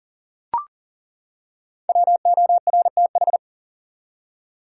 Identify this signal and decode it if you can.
Morse code, heard as a keyed tone, about 37 wpm, 700 Hz. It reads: WORTH
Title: Five Letter Words - 40wpm